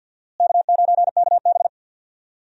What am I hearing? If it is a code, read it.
Morse code, heard as a keyed tone, about 53 wpm, 700 Hz. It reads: K9KB
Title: USA Call Signs - 4 Characters - Repeated 3x - 50wpm